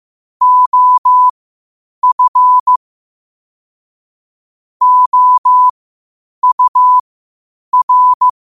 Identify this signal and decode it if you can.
Morse code, heard as a keyed tone, about 15 wpm, spaced as for 8 wpm, 1000 Hz. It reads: OF OUR
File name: qotdfiles / 8wpm